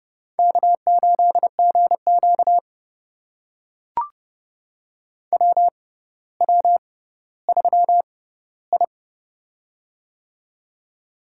Morse code, heard as a keyed tone, about 32 wpm, spaced as for 16 wpm, 700 Hz. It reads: K8GQ WW3I
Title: USA Call Signs - 4 Characters - 6x Character Spacing - 30wpm